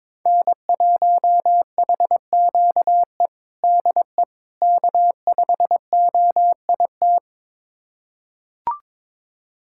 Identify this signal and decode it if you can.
Morse code, heard as a keyed tone, about 22 wpm, 700 Hz. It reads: N1HQE DE K5OIT
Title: USA Callsign DE Callsign - 5 Characters - 22wpm